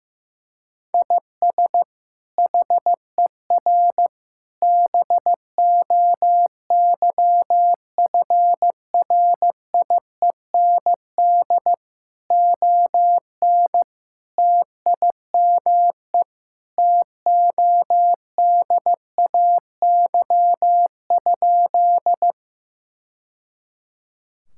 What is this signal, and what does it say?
Morse code, heard as a keyed tone, about 15 wpm, 700 Hz. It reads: IS HER BOYFRIEND ON TIME TODAY?
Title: Sentences from Top 1000 Words - Review Part 5 - 15wpm